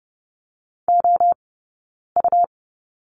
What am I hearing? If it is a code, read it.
Morse code, heard as a keyed tone, about 30 wpm, spaced as for 9 wpm, 690 Hz. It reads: OU
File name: Lesson 19d - Letter w - Two Characters 30wpm